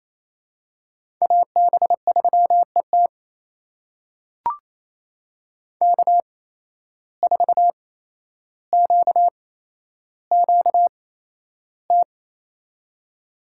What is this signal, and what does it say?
Morse code, heard as a keyed tone, about 28 wpm, spaced as for 12 wpm, 700 Hz. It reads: AB3ET K4QQT
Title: USA Call Signs - 5 Characters - 8x Character Spacing - 28wpm